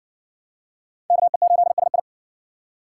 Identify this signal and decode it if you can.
Morse code, heard as a keyed tone, about 53 wpm, 700 Hz. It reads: KE8SI